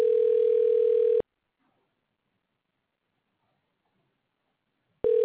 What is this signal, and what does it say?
Ringback tone (North America), 440 + 480 Hz.
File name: recordings / RE85fce09c545c a1a23719e00e8